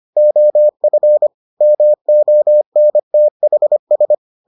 Morse code, heard as a keyed tone, about 25 wpm, 600 Hz. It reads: OF MONTHS